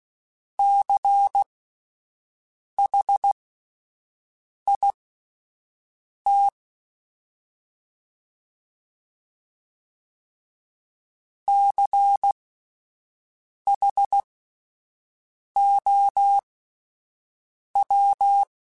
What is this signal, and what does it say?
Morse code, heard as a keyed tone, about 16 wpm, spaced as for 5 wpm, 800 Hz. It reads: CHIT CHOW
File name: Alphabetical / Alphabetical mp3s / Cs @5